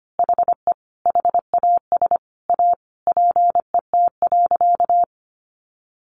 Morse code, heard as a keyed tone, about 25 wpm, 700 Hz. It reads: HE HAS A PET.
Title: Easy Phrases 25wpm